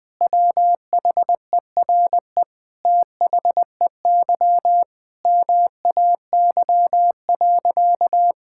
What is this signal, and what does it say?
Morse code, heard as a keyed tone, about 20 wpm, 700 Hz. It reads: WHERE THEY MAY.
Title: Sentences with 8-Word Phrases - Spoken Beginning - 20wpm